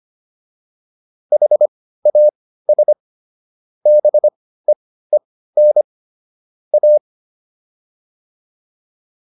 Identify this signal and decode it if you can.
Morse code, heard as a keyed tone, about 25 wpm, spaced as for 15 wpm, 600 Hz. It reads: HAS BEEN A
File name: WD205 15